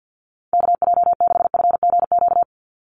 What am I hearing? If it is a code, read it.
Morse code, heard as a keyed tone, about 48 wpm, 700 Hz. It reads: KJ6FGQ
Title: USA Call Signs 50wpm